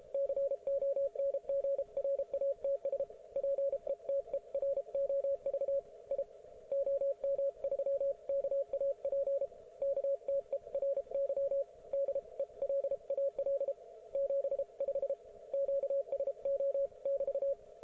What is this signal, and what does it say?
Morse code, heard as a keyed tone, about 33 wpm, 550 Hz, with radio noise. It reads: CONGRATS PETEROV I OM3KAP KTERY DELAL 75 QSO<BT>
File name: Archiv / CUC157 / ok1fhi cul157